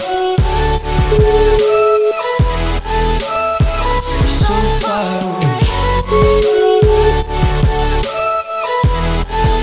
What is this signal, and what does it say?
Ringback tone (Europe, 425 Hz), with noise.